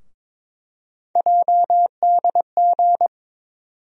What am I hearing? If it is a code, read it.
Morse code, heard as a keyed tone, about 22 wpm, 700 Hz. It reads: JDG
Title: USA Call Sign Postfixes - 3 Letters 22wpm